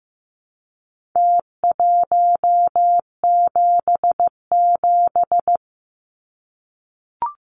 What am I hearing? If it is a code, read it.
Morse code, heard as a keyed tone, about 15 wpm, 700 Hz. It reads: T177
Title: Numbers - Contest Style - 4-Digit 15wpm